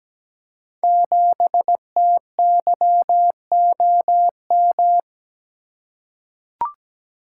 Morse code, heard as a keyed tone, about 17 wpm, 700 Hz. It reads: 7TYOM
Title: Code Groups - 5 Letter:Numbers - 17wpm